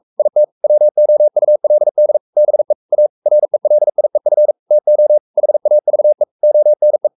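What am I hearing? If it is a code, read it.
Morse code, heard as a keyed tone, about 44 wpm, 600 Hz. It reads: IT WOULD BE A RELIEF TO HAVE ONE
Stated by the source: Morse code at 43 wpm sending Ken Follett's novel Never in English, from CW med Gnister.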